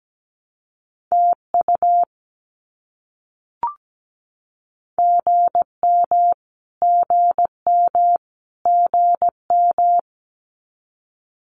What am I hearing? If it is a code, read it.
Morse code, heard as a keyed tone, about 17 wpm, 700 Hz. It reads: TU GM GM GM
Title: Abbreviations Repeated 3x - 17wpm